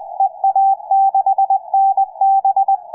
Morse code, heard as a keyed tone, about 19 wpm, 780 Hz, with radio noise. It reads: EA6NB
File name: EA6NB-160-CW-4ABR15